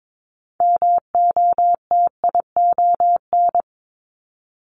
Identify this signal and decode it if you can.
Morse code, heard as a keyed tone, about 22 wpm, 700 Hz. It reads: MOTION
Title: Multi-letter Phonemes 22wpm